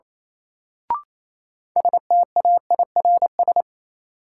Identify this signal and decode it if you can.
Morse code, heard as a keyed tone, about 28 wpm, 700 Hz. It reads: STAIRS
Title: Sets of 5 Words Encore - New Word Drill - 28wpm